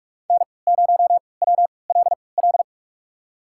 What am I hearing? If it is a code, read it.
Morse code, heard as a keyed tone, about 45 wpm, spaced as for 25 wpm, 700 Hz. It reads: N0WLL
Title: USA Call Signs - 3x Character Spacing - 45wpm